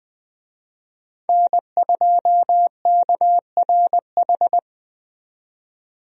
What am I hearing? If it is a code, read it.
Morse code, heard as a keyed tone, about 20 wpm, 700 Hz. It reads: N2KRH
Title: USA Call Signs - 5 Characters - 20wpm